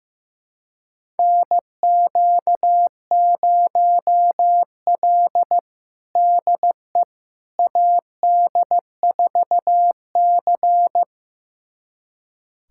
Morse code, heard as a keyed tone, about 15 wpm, 700 Hz. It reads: NQ0L DE AD4C
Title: USA Callsign DE Callsign - 4 Characters - 15wpm